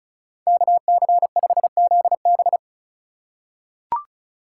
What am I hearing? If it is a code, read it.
Morse code, heard as a keyed tone, about 36 wpm, 700 Hz. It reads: KC5ZB
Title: USA Call Signs - 5 Characters - 2x Character Spacing - 35wpm